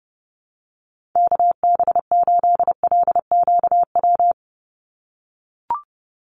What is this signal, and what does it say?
Morse code, heard as a keyed tone, about 29 wpm, 700 Hz. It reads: KB8LQW
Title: USA Call Signs 30wpm